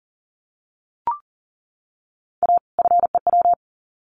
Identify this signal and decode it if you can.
Morse code, heard as a keyed tone, about 38 wpm, 700 Hz. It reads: A FEW
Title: Sets of 2 Words 40wpm